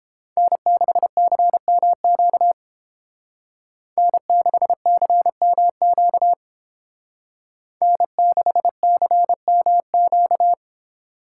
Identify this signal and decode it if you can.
Morse code, heard as a keyed tone, about 30 wpm, 700 Hz. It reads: N6CMQ N6CMQ N6CMQ